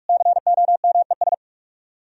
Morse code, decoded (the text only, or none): COMES